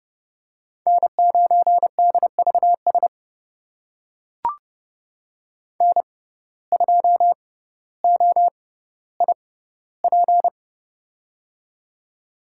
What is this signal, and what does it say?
Morse code, heard as a keyed tone, about 32 wpm, spaced as for 16 wpm, 700 Hz. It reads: N9DVS N2OIP
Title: USA Call Signs - 5 Characters - 6x Character Spacing - 30wpm